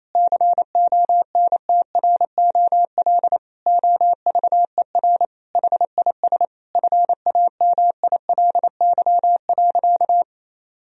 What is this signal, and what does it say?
Morse code, heard as a keyed tone, about 28 wpm, 700 Hz. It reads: CONTROL OVER HIS FAMILY.
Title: Sentences from Top 1000 Words - Review Part 3 - 28wpm